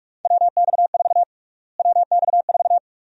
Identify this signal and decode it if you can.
Morse code, heard as a keyed tone, about 42 wpm, 700 Hz. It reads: WX4 WX4
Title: USA Call Sign Prefixes - Repeated 2x - 45wpm